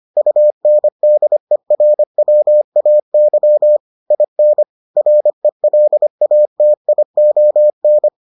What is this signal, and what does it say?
Morse code, heard as a keyed tone, about 25 wpm, 600 Hz. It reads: UNDERWAY IN RELATION